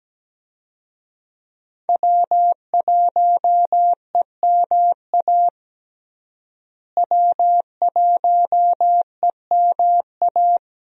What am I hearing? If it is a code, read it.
Morse code, heard as a keyed tone, about 17 wpm, 700 Hz. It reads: W1EMA W1EMA
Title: USA Call Signs - 5 Characters - Repeated 3x - 17wpm